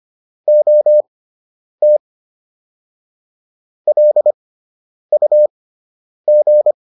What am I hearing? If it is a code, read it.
Morse code, heard as a keyed tone, about 25 wpm, spaced as for 9 wpm, 600 Hz. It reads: OT LUG